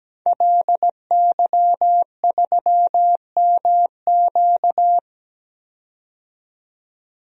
Morse code, heard as a keyed tone, about 17 wpm, 700 Hz. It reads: LY3MQ
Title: DXCC-100 Call Signs - 5 Characters - 17wpm